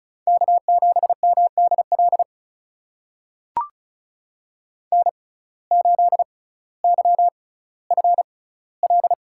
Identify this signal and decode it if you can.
Morse code, heard as a keyed tone, about 35 wpm, spaced as for 18 wpm, 700 Hz. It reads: K7MDL N8YFL
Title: USA Call Signs - 5 Characters - 6x Character Spacing - 35wpm